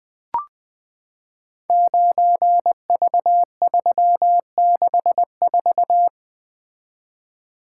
Morse code, heard as a keyed tone, about 20 wpm, 700 Hz. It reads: 9V364
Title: Code Groups - 5 Letter:Numbers - 20wpm